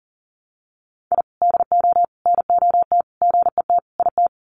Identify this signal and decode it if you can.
Morse code, heard as a keyed tone, about 40 wpm, 700 Hz. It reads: I DO NOT GET IT
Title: Sets of 5 Words Encore 40wpm